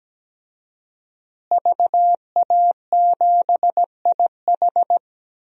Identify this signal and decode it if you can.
Morse code, heard as a keyed tone, about 17 wpm, 700 Hz. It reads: VA7IH
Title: DXCC-100 Call Signs - 5 Characters - Repeated 2x - 17wpm